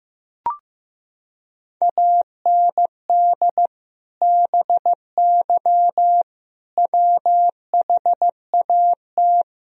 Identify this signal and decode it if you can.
Morse code, heard as a keyed tone, about 15 wpm, 700 Hz. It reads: AND BY WHAT
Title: Sentences from Top 1000 Words - Review Part 3 - 15wpm